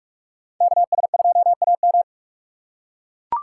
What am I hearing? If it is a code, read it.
Morse code, heard as a keyed tone, about 43 wpm, 700 Hz. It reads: KS2AM